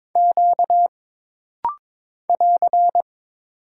Morse code, heard as a keyed tone, about 22 wpm, 700 Hz. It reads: Q <AR>
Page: This rapid-fire timing is geared to push your ICR (Instant Character Recognition) skills.